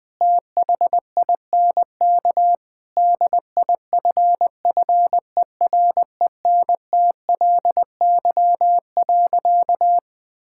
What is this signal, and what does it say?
Morse code, heard as a keyed tone, about 20 wpm, 700 Hz. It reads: THINK DIFFERENTLY.
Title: Sentences with 3-Word Phrases - 20wpm